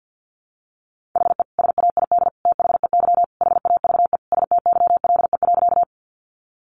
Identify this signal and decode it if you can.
Morse code, heard as a keyed tone, about 50 wpm, 700 Hz. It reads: HE SAID THEY HAVE STYLE.
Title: Sentences from Top 800 Words 50wpm